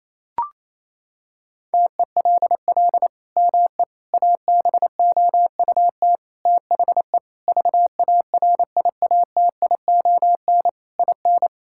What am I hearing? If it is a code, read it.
Morse code, heard as a keyed tone, about 28 wpm, 700 Hz. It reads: TELL ME ABOUT THE VARIATION IN
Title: Sentences from Top 1600 Words 28wpm